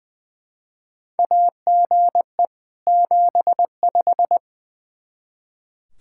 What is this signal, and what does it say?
Morse code, heard as a keyed tone, about 20 wpm, 700 Hz. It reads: AGE 75
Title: K1USN SST QSO Practice 20wpm